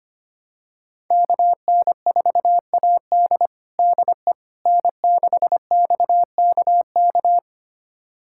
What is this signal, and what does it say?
Morse code, heard as a keyed tone, about 25 wpm, 700 Hz. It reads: KN4AD DE N6XKK
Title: USA Callsign DE Callsign - 5 Characters - 25wpm